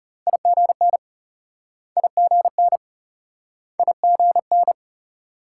Morse code, heard as a keyed tone, about 38 wpm, 700 Hz. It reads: IGN IGN IGN